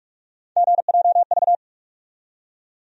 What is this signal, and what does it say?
Morse code, heard as a keyed tone, about 46 wpm, 700 Hz. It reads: GJV
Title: USA Call Sign Postfixes - 3 Letters 45wpm